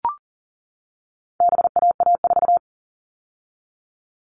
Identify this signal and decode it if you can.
Morse code, heard as a keyed tone, about 40 wpm, 700 Hz. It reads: BAA4
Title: All Cut Numbers - 4-Digit 40wpm